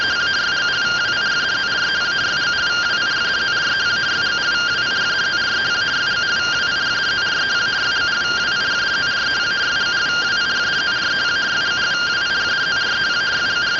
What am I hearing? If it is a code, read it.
Two-tone teleprinter signal, about 170 Hz shift.
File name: findings / CT 1 Scout 0001.gen-15 rtty rx